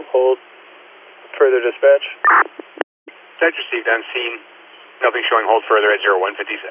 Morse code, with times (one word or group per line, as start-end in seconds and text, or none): none